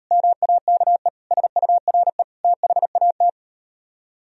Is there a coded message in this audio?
Yes: Morse code from 0.1 to 3.3 s, heard as a keyed tone, about 38 wpm, 700 Hz. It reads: MAKE SURE THAT